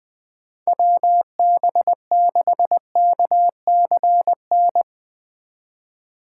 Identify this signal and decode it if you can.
Morse code, heard as a keyed tone, about 20 wpm, 700 Hz. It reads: WB6KCN
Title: USA Call Signs - 6 Characters - 6x Character Spacing - 20wpm